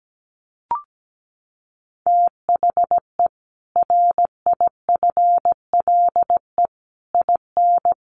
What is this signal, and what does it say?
Morse code, heard as a keyed tone, about 17 wpm, 700 Hz. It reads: THE RIFLE IN
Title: A Sound Of Thunder 17wpm